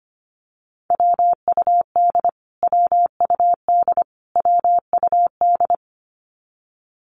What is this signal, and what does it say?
Morse code, heard as a keyed tone, about 25 wpm, 700 Hz. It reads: WUD WUD WUD
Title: Abbreviations Repeated 3x - 25wpm